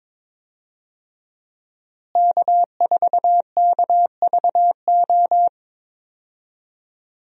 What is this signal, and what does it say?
Morse code, heard as a keyed tone, about 22 wpm, 700 Hz. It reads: K4KVO
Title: USA Call Signs - 5 Characters - Repeated 3x - 22wpm